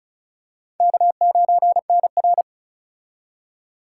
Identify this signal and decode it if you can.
Morse code, heard as a keyed tone, about 36 wpm, 700 Hz. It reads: K9NR
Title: K1USN SST QSO Practice - 35wpm